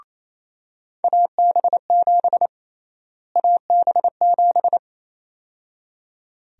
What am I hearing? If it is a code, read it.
Morse code, heard as a keyed tone, about 28 wpm, 700 Hz. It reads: AB7 AB7
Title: USA Call Sign Prefixes - Repeated 2x - 28wpm